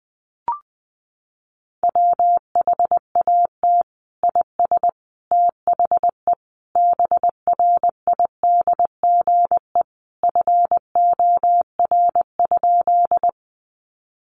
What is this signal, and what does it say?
Morse code, heard as a keyed tone, about 20 wpm, 700 Hz. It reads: WHAT IS THE BRIDGE FOR?
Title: Sentences from Top 1200 Words 20wpm